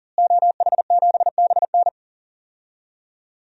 Morse code, heard as a keyed tone, about 40 wpm, 700 Hz. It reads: OH7BN
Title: DXCC-100 Call Signs - 5 Characters - Repeated 2x - 40wpm